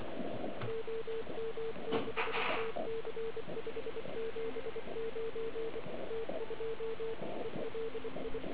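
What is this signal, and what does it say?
Morse code, heard as a keyed tone, about 26 wpm, 450 Hz, with radio noise. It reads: OM6TC 579 T2 ELI
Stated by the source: Morse from the OK QRP 2013 contest, run at 1 watt on a 12.3V lithium battery, Liberci, with OK1MKX also at 1W.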